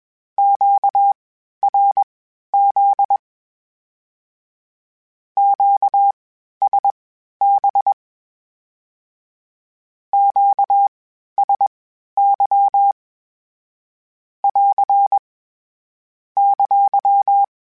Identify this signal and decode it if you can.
Morse code, heard as a keyed tone, about 21 wpm, spaced as for 12 wpm, 800 Hz. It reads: QRZ QSB QSY <AR> !
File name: (spaced) / Q-Calls Spaced-mp3s / Q Calls sp Once @20